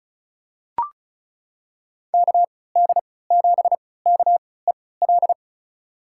Morse code, heard as a keyed tone, about 36 wpm, spaced as for 20 wpm, 700 Hz. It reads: KD7KEL
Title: USA Call Signs - 6 Characters - 3x Character Spacing - 35wpm